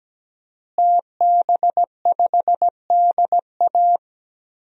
Morse code, heard as a keyed tone, about 17 wpm, 700 Hz. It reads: TB5DA
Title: DXCC-100 Call Signs - 5 Characters - Repeated 2x - 17wpm